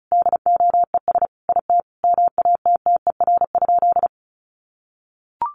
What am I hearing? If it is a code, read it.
Morse code, heard as a keyed tone, about 34 wpm, 700 Hz. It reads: DOES IT MATTER?
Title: Sentences from Top 700 Words 35wpm